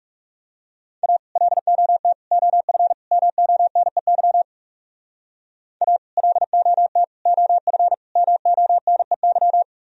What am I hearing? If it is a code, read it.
Morse code, heard as a keyed tone, about 38 wpm, 700 Hz. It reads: A LOT OF MONEY A LOT OF MONEY